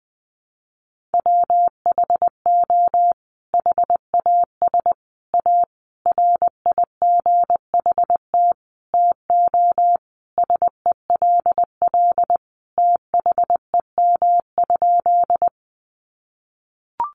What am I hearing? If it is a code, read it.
Morse code, heard as a keyed tone, about 20 wpm, 700 Hz. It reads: WHO HAS A RIGHT TO SELL THEM?
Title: Sentences from Top 1100 Words 20wpm